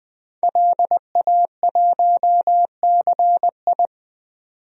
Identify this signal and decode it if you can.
Morse code, heard as a keyed tone, about 20 wpm, 700 Hz. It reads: LA1CI
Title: DXCC-100 Call Signs - 5 Characters - 20wpm